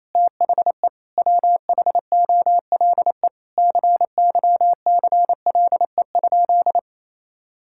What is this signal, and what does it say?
Morse code, heard as a keyed tone, about 27 wpm, 700 Hz. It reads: THE WHOLE CYCLE?
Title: Sentences from Top 1000 Words - Review Part 6 - 28wpm